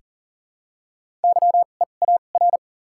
Morse code, heard as a keyed tone, about 40 wpm, 700 Hz. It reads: Y E A R
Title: 2x Character Spacing - Top 300 Words - 40wpm